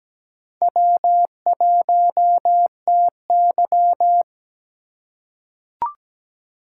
Morse code, heard as a keyed tone, about 17 wpm, 700 Hz. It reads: W1TY
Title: USA Call Signs - 4 Characters - 8x Character Spacing - 17wpm